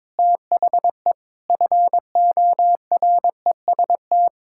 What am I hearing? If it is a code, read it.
Morse code, heard as a keyed tone, about 22 wpm, 700 Hz. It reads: THE FOREST
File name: practice / 22wpm / Sentences with 4-Word Phrases - 22wpm